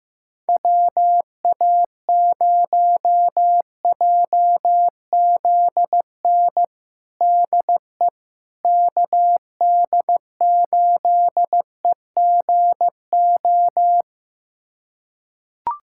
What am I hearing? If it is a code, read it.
Morse code, heard as a keyed tone, about 15 wpm, 700 Hz. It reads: WA0JZN DE KD8EGO
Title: USA Callsign DE Callsign 15wpm